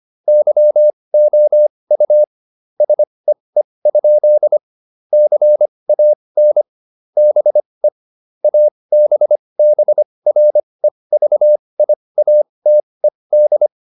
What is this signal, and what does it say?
Morse code, heard as a keyed tone, about 25 wpm, spaced as for 20 wpm, 600 Hz. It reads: YOU SEE? CAN BE ABBREVIATED